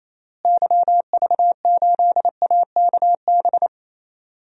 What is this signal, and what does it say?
Morse code, heard as a keyed tone, about 28 wpm, 700 Hz. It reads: YV8AKB
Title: DXCC-100 Call Signs - 6 Characters - Repeated 3x - 28wpm